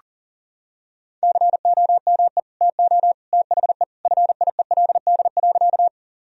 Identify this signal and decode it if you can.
Morse code, heard as a keyed tone, about 40 wpm, 700 Hz. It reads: COME TO THE FIELD.
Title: Sentences from Top 1000 Words - Review Part 2 - 40wpm